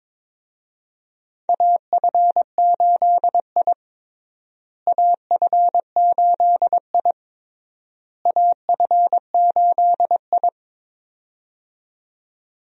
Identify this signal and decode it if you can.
Morse code, heard as a keyed tone, about 22 wpm, 700 Hz. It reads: AF8I AF8I AF8I
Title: USA Call Signs - 4 Characters - Repeated 3x - 22wpm